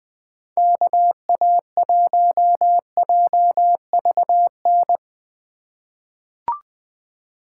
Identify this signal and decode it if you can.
Morse code, heard as a keyed tone, about 20 wpm, 700 Hz. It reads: KA1JVN
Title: USA Call Signs - 6 Characters - 2x Character Spacing - 20wpm